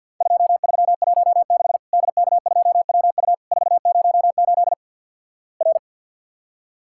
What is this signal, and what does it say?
Morse code, heard as a keyed tone, about 50 wpm, 700 Hz. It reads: 2316 DK2WU 408 R
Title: Worked All Europe DX Contest - QTC Exchange with Repeats - 50wpm